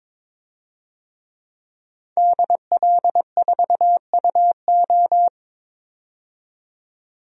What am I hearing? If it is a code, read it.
Morse code, heard as a keyed tone, about 22 wpm, 700 Hz. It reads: DL4UO
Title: DXCC-100 Call Signs - 5 Characters - 22wpm